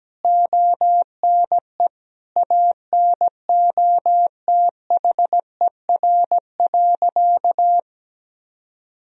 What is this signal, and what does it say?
Morse code, heard as a keyed tone, about 17 wpm, 700 Hz. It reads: ONE ANOTHER.
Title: Sentences from Top 1000 Words - Review Part 1 - 17wpm